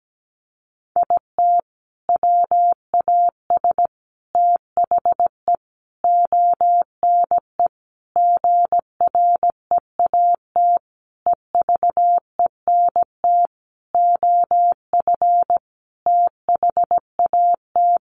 Morse code, heard as a keyed tone, about 17 wpm, 700 Hz. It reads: IT WAS THE ONE GREAT EVENT OF THAT
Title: Sentences from Top 1400 Words 17wpm